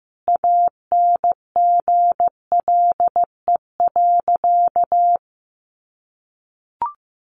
Morse code, heard as a keyed tone, about 15 wpm, 700 Hz. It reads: ANGLE.
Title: Sentences from Top 1500 Words 15wpm